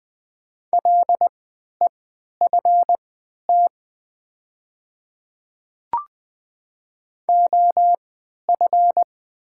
Morse code, heard as a keyed tone, about 20 wpm, spaced as for 11 wpm, 700 Hz. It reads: LEFT OF
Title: 3x Character Spacing - Top 300 Words - 20wpm